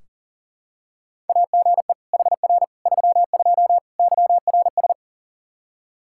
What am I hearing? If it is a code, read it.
Morse code, heard as a keyed tone, about 40 wpm, 700 Hz. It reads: AGE HR 32 YRS
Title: Practice QSO - Exchange 3 - More Info - 40wpm